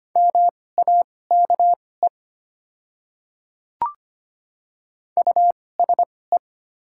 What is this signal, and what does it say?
Morse code, heard as a keyed tone, about 25 wpm, spaced as for 18 wpm, 700 Hz. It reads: MAKE USE
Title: 2x Character Spacing - Top 100 Words - 25wpm